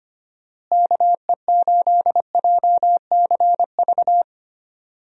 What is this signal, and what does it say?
Morse code, heard as a keyed tone, about 25 wpm, 700 Hz. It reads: KE8JCV
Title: USA Call Signs - 6 Characters - Repeated 2x - 25wpm